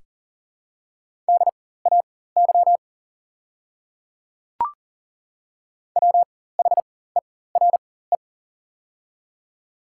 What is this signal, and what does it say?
Morse code, heard as a keyed tone, about 40 wpm, spaced as for 19 wpm, 700 Hz. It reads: DAY WHERE
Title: 4x Character Spacing - Top 200 Words - 40wpm